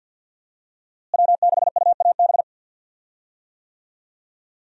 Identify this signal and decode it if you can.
Morse code, heard as a keyed tone, about 50 wpm, 700 Hz. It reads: W6UAB
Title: USA Call Signs - 5 Characters - Repeated 2x - 50wpm